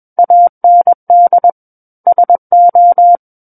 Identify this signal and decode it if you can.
Morse code, heard as a keyed tone, about 21 wpm, 700 Hz. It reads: AND SO
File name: Book One / 12-Chapter 12 - 21 WPM